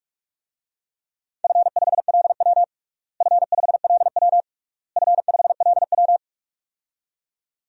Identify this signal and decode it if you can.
Morse code, heard as a keyed tone, about 46 wpm, 700 Hz. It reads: F5LW F5LW F5LW
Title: DXCC-100 Call Signs - 4 Characters - Repeated 3x - 45wpm